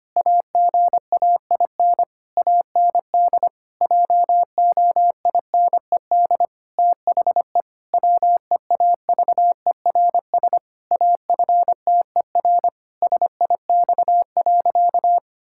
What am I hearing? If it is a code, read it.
Morse code, heard as a keyed tone, about 25 wpm, 700 Hz. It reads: AGAIN AND JOINED THE WEAVERS AFTER SIX.
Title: Pangram Challenge v2 - 25wpm